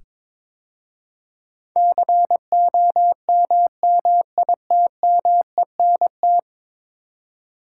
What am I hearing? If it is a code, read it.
Morse code, heard as a keyed tone, about 22 wpm, 700 Hz. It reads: COMMITMENT
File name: practice / 22wpm / 10 Letter Words - First 3 Letters Given - 22wpm